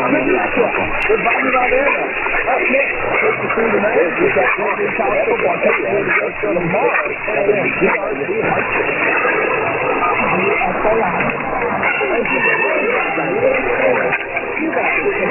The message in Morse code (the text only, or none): none